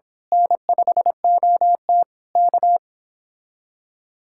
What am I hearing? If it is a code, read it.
Morse code, heard as a keyed tone, about 25 wpm, 700 Hz. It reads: N5OT K
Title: Speed-Racing Common QSO Phrases - 22wpm